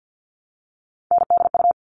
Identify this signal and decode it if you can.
Morse code, heard as a keyed tone, about 48 wpm, 700 Hz. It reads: NDU
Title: All Cut Numbers - 3-Digit 50wpm